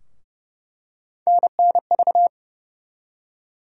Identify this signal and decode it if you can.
Morse code, heard as a keyed tone, about 31 wpm, 700 Hz. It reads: NNV
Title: USA Call Sign Postfixes - 3 Letters 30wpm